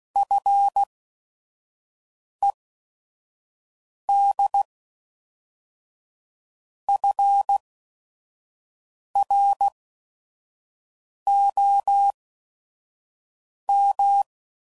Morse code, heard as a keyed tone, about 16 wpm, spaced as for 5 wpm, 800 Hz. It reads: FED FROM
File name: CW Code Groups @5WPM / Technical Subjects / SwPwrSup